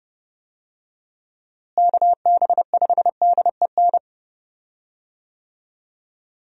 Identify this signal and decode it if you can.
Morse code, heard as a keyed tone, about 30 wpm, 700 Hz. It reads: KB5DEN
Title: USA Call Signs - 6 Characters - Repeated 2x - 30wpm